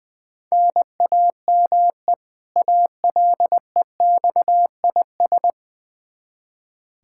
Morse code, heard as a keyed tone, about 20 wpm, 700 Hz. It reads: NAME ALEXIS
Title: Practice QSO - Exchange 1 - Beginning - 20wpm